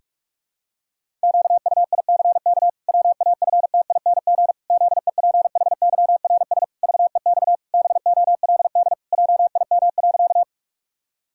Morse code, heard as a keyed tone, about 46 wpm, 700 Hz. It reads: QUICK WAFTING ZEPHYRS VEX BOLD JIM.
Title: Pangram Challenge v2 - 45wpm